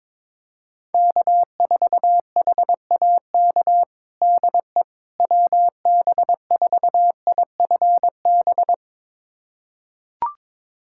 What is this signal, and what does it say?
Morse code, heard as a keyed tone, about 22 wpm, 700 Hz. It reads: K4HAK DE WB4IFB
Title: USA Callsign DE Callsign 22wpm